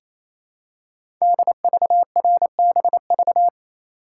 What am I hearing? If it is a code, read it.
Morse code, heard as a keyed tone, about 29 wpm, 690 Hz. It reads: DVRBV
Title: Code Groups - 5 Letters - 28wpm